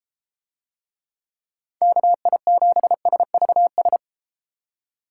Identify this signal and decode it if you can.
Morse code, heard as a keyed tone, about 30 wpm, 700 Hz. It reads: KI7SVS